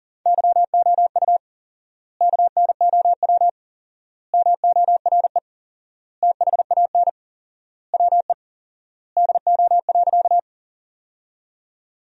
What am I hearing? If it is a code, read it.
Morse code, heard as a keyed tone, about 40 wpm, 700 Hz. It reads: YOU KNOW MORE THAN WE DO.
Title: 4x Word-Spacing - Sentences from Top 100 Words - 40wpm